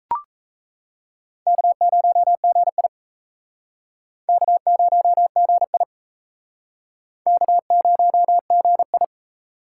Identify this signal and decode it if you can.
Morse code, heard as a keyed tone, about 35 wpm, 700 Hz. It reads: K0GI K0GI K0GI
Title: Speed Racing - USA 1x2 Call Signs 28wpm